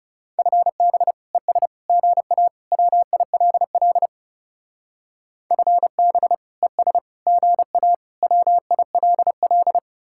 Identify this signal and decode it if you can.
Morse code, heard as a keyed tone, about 38 wpm, 700 Hz. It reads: FB ES GA WILL FB ES GA WILL